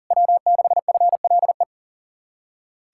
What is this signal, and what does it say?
Morse code, heard as a keyed tone, about 52 wpm, 700 Hz. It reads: W6FLE